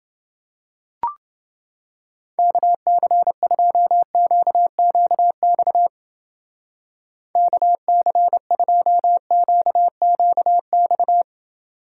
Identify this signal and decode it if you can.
Morse code, heard as a keyed tone, about 29 wpm, 700 Hz. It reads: KC2QQX KC2QQX